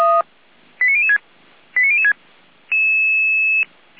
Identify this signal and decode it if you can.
Five-tone selective call (ZVEI), heard as quick runs of five tones, tones of about 70 ms. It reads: 89006 89006
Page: ZVEI